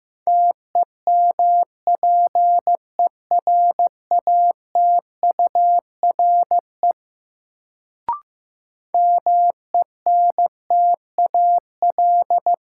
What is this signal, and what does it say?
Morse code, heard as a keyed tone, about 16 wpm, 700 Hz. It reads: TEMPERATURE MENTAL
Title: Sentences from Top 1000 Words - Review Part 4 - New Word Drill - 15wpm